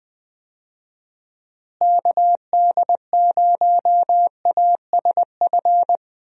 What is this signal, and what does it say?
Morse code, heard as a keyed tone, about 20 wpm, 700 Hz. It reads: KD0ASF
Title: USA Call Signs - 6 Characters - Repeated 3x - 20wpm